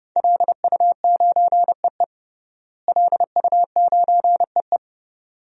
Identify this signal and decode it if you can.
Morse code, heard as a keyed tone, about 31 wpm, 700 Hz. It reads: LU9EE LU9EE
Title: DXCC-100 Call Signs - 5 Characters - Repeated 2x - 30wpm